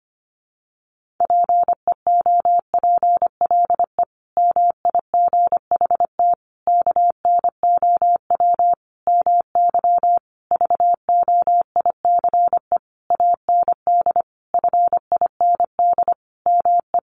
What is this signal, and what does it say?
Morse code, heard as a keyed tone, about 25 wpm, 700 Hz. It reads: PEOPLE MIGHT KNOW MY VOICE AND FIND ME
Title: Sentences from Top 500 Words 25wpm